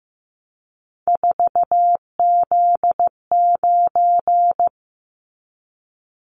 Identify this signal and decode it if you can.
Morse code, heard as a keyed tone, about 15 wpm, 700 Hz. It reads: 4Z9
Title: Callsign Prefixes for Top 50 DXCC - 15wpm